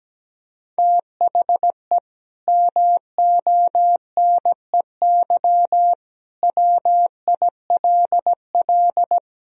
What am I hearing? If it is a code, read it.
Morse code, heard as a keyed tone, about 17 wpm, 700 Hz. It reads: THE MONEY WILL
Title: Sentences from Top 1000 Words - Review Part 4 - 17wpm